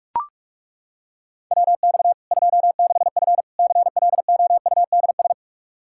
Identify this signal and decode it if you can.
Morse code, heard as a keyed tone, about 41 wpm, 700 Hz. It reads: WX 26F CLOUDS
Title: Practice QSO - Exchange 2 - Info - 45wpm